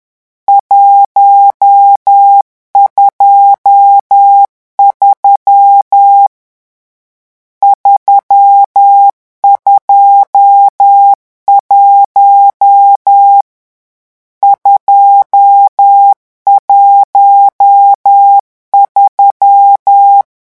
Morse code, heard as a keyed tone, about 10 wpm, 800 Hz. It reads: 123 321 213
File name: morse mp3 / lecon10-vitesse 10